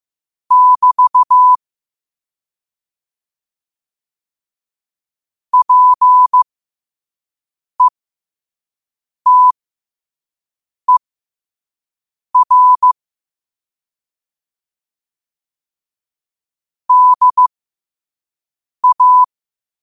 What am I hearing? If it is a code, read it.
Morse code, heard as a keyed tone, about 15 wpm, spaced as for 5 wpm, 1000 Hz. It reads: <BT> PETER DA
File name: qotdfiles / 5wpm